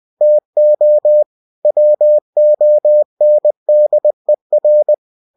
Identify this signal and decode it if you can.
Morse code, heard as a keyed tone, about 20 wpm, 600 Hz. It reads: TO WONDER
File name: cw-books / winnie-the-pooh / winnie-20wpm-page-0103